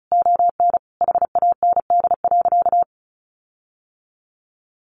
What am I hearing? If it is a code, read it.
Morse code, heard as a keyed tone, about 34 wpm, 700 Hz. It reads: ON HAND.
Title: Sentences from Top 1500 Words 35wpm